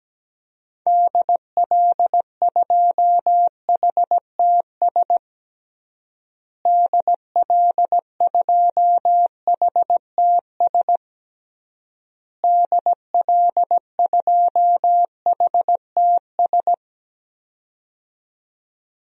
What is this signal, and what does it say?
Morse code, heard as a keyed tone, about 17 wpm, 700 Hz. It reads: DL2HTS DL2HTS DL2HTS
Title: DXCC-100 Call Signs - 6 Characters - Repeated 3x - 17wpm